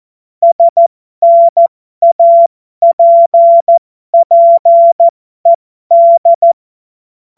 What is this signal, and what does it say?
Morse code, heard as a keyed tone, about 14 wpm, spaced as for 12 wpm, 680 Hz. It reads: SNAPPED
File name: Morsekurs / Witze